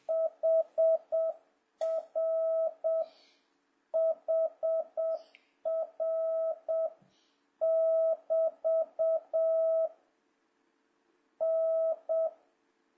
Morse code, heard as a keyed tone, about 7 wpm, 650 Hz. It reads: HR HR <BT> N
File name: jh8cbh / 2026cw